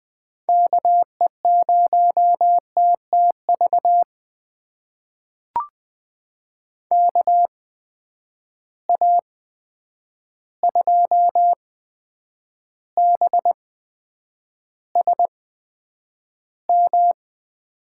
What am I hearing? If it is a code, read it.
Morse code, heard as a keyed tone, about 20 wpm, spaced as for 9 wpm, 700 Hz. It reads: KE0TTV KA2BSM
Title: USA Call Signs - 6 Characters - 8x Character Spacing - 20wpm